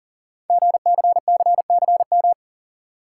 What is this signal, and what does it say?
Morse code, heard as a keyed tone, about 38 wpm, 700 Hz. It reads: GCCCM
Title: Code Groups - 5 Letters - 40wpm